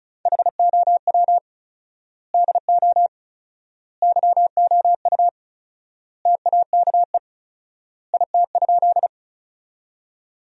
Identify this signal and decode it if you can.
Morse code, heard as a keyed tone, about 34 wpm, 700 Hz. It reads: HOW DO YOU TAKE IT?
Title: 4x Word-Spacing - Sentences from Top 200 Words - 35wpm